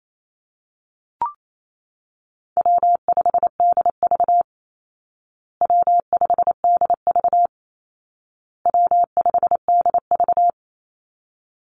Morse code, heard as a keyed tone, about 28 wpm, 700 Hz. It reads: W5DV W5DV W5DV
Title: Call Signs Repeated 2x - 28wpm